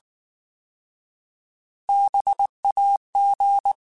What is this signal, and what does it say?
Morse code, heard as a keyed tone, about 19 wpm, 800 Hz. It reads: BAG
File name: Alphabetical / Alphabetical mp3s / Bs @18